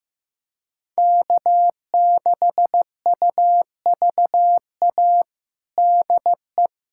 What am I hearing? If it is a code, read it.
Morse code, heard as a keyed tone, about 15 wpm, 700 Hz. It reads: K6UVA DE
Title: USA Callsign DE Callsign - 5 Characters - 15wpm